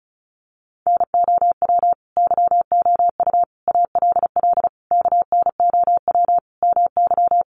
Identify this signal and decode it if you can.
Morse code, heard as a keyed tone, about 34 wpm, 700 Hz. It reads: NOW YOU ALL KNOW MY
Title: Sentences from Top 1200 Words 35wpm